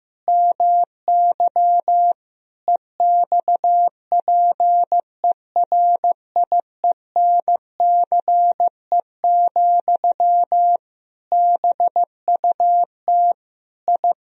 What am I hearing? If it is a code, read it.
Morse code, heard as a keyed tone, about 15 wpm, 700 Hz. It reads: MY EXPERIENCE, BUT I
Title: Sentences from Top 1000 Words - Review Part 4 - 15wpm